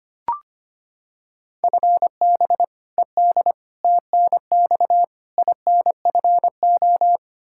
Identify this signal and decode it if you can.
Morse code, heard as a keyed tone, about 25 wpm, 700 Hz. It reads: FB ED TNX INFO
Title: K1USN SST QSO Practice 25wpm